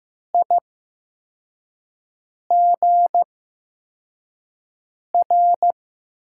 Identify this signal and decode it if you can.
Morse code, heard as a keyed tone, about 15 wpm, spaced as for 4 wpm, 700 Hz. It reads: IGR
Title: USA Call Signs - 6 Characters - 8x Character Spacing - 15wpm